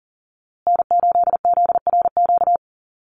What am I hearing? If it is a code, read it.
Morse code, heard as a keyed tone, about 40 wpm, 700 Hz. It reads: N8ZRQ
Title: USA Call Signs Repeated 3x 40wpm